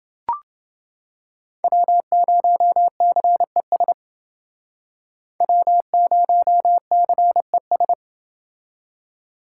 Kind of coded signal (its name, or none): Morse code